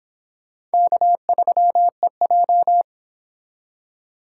Morse code, heard as a keyed tone, about 26 wpm, 700 Hz. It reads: K3EJ